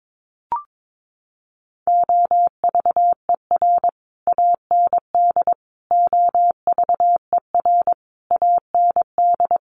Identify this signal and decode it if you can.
Morse code, heard as a keyed tone, about 22 wpm, 700 Hz. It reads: OVER AND OVER AND
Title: Sets of 5 Words Encore 22wpm